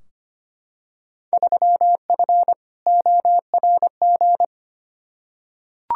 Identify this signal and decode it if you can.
Morse code, heard as a keyed tone, about 25 wpm, 700 Hz. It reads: 3F ORG
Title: Field Day Exchanges - 25wpm